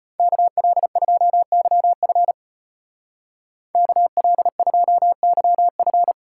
Morse code, heard as a keyed tone, about 30 wpm, 700 Hz. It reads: KL2YF KL2YF
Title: Speed Racing - USA 2x2 Call Signs 25wpm